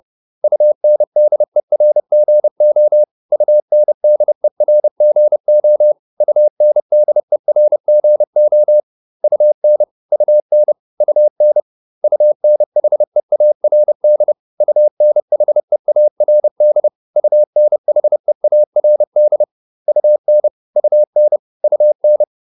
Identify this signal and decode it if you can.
Morse code, heard as a keyed tone, about 30 wpm, 600 Hz. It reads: UNDERGO UNDERGO UNDERGO UN UN UN UNHEARD UNHEARD UNHEARD UN UN UN